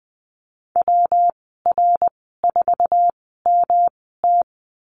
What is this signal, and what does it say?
Morse code, heard as a keyed tone, about 20 wpm, spaced as for 15 wpm, 700 Hz. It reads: WR4MT
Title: 167 Lesson 27g - Number 7 - Call Signs 20wpm